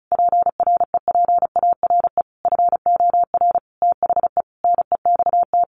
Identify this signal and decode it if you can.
Morse code, heard as a keyed tone, about 34 wpm, 700 Hz. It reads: PREPARE FOR THE NEXT
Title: Sentences from Top 1200 Words 35wpm